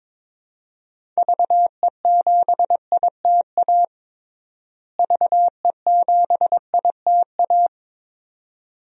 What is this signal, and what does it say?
Morse code, heard as a keyed tone, about 21 wpm, 700 Hz. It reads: VE7ITA VE7ITA
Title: DXCC-100 Call Signs - 6 Characters - Repeated 2x - 22wpm